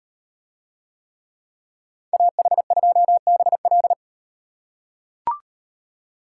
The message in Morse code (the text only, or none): AH2BL